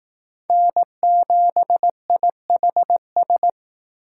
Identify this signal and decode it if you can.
Morse code, heard as a keyed tone, about 18 wpm, 700 Hz. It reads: N7IHS
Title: Speed Racing - USA 1x3 Call Signs 15wpm